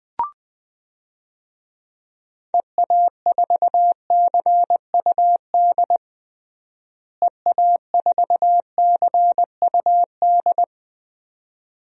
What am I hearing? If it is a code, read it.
Morse code, heard as a keyed tone, about 21 wpm, 700 Hz. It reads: EA4CUD EA4CUD
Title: DXCC-100 Call Signs - 6 Characters - Repeated 2x - 20wpm